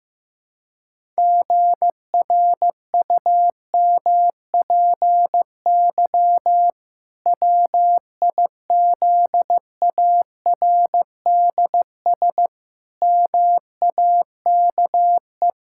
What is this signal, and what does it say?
Morse code, heard as a keyed tone, about 15 wpm, 700 Hz. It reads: GRUMPY WIZARDS MAKE
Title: Pangram Challenge v2 - 15wpm